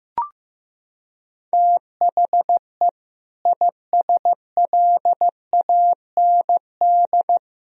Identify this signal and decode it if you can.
Morse code, heard as a keyed tone, about 16 wpm, 700 Hz. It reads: THE ISLAND
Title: Sentences from Top 1000 Words - Review Part 3 - 15wpm